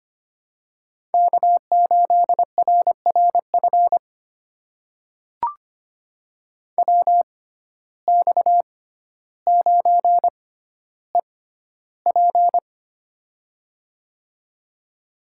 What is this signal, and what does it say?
Morse code, heard as a keyed tone, about 25 wpm, spaced as for 13 wpm, 700 Hz. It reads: K8RRF WX9EP
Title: USA Call Signs - 5 Characters - 6x Character Spacing - 25wpm